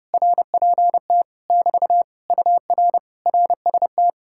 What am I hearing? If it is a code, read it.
Morse code, heard as a keyed tone, about 30 wpm, 700 Hz. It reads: RPT <BT> UR RST